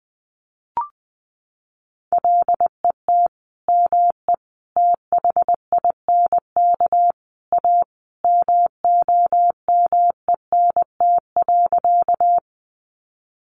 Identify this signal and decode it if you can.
Morse code, heard as a keyed tone, about 20 wpm, 700 Hz. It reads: LET ME THINK A MOMENT.
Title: Sentences from Top 1000 Words 20wpm